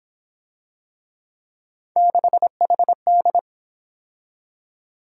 Morse code, heard as a keyed tone, about 29 wpm, 700 Hz. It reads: 6HD